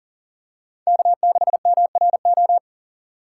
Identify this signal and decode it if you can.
Morse code, heard as a keyed tone, about 40 wpm, 700 Hz. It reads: K6MRO
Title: USA Call Signs - 5 Characters - 4x Character Spacing - 40wpm